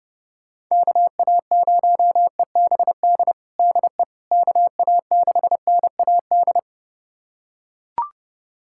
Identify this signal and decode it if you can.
Morse code, heard as a keyed tone, about 30 wpm, 700 Hz. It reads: KA0EBD DE KA6NAD
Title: USA Callsign DE Callsign - 6 Characters - 30wpm